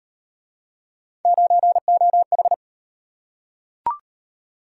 Morse code, heard as a keyed tone, about 38 wpm, 700 Hz. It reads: 9OH